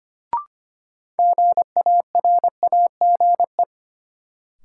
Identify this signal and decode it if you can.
Morse code, heard as a keyed tone, about 25 wpm, 700 Hz. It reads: GARAGE
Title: Sentences from Top 1600 Words - New Word Drill - 25wpm